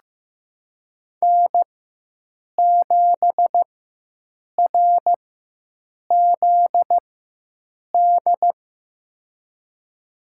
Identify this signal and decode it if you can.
Morse code, heard as a keyed tone, about 15 wpm, spaced as for 7 wpm, 700 Hz. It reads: N7RZD
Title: USA Call Signs - 4x Character Spacing - 15wpm